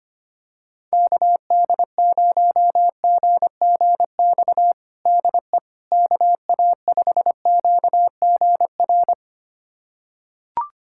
Morse code, heard as a keyed tone, about 25 wpm, 700 Hz. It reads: KD0GGX DE KA5QGR
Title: USA Callsign DE Callsign 25wpm